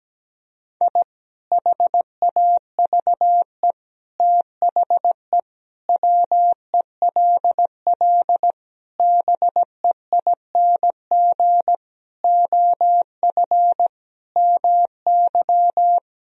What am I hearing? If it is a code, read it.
Morse code, heard as a keyed tone, about 17 wpm, 700 Hz. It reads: I HAVE THE WELL BEING OF MY
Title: Sentences from Top 1000 Words - Review Part 3 - 17wpm